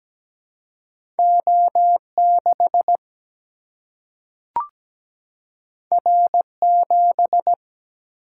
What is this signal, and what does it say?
Morse code, heard as a keyed tone, about 17 wpm, 700 Hz. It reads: O6 R7
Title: Single-Letter Single-Number 17wpm